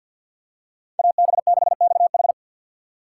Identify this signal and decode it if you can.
Morse code, heard as a keyed tone, about 46 wpm, 700 Hz. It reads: AB6XH
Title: USA Call Signs - 3x Character Spacing - 50wpm